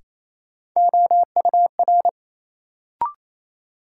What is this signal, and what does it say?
Morse code, heard as a keyed tone, about 28 wpm, 700 Hz. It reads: OUR
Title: Sets of 2 Words - New Word Drill - 28wpm